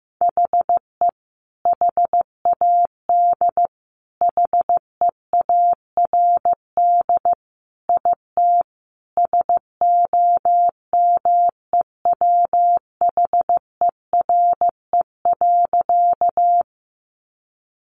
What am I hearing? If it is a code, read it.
Morse code, heard as a keyed tone, about 15 wpm, 700 Hz. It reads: HE HAD HEARD IT SOMEWHERE.
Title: Sentences from Top 1400 Words 15wpm